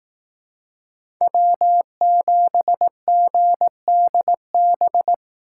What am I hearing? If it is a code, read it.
Morse code, heard as a keyed tone, about 18 wpm, 700 Hz. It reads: W7GDB
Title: Speed Racing - USA Call Signs 15wpm